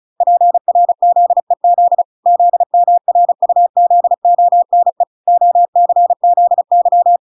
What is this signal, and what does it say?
Morse code, heard as a keyed tone, about 34 wpm, 700 Hz. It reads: PRZEZ ZMRUZONE OCZY